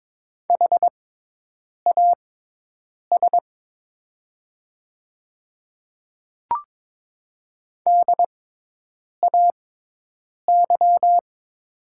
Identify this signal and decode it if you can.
Morse code, heard as a keyed tone, about 23 wpm, spaced as for 8 wpm, 700 Hz. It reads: HAS DAY
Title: 6x Character Spacing - Top 100 Words - 22wpm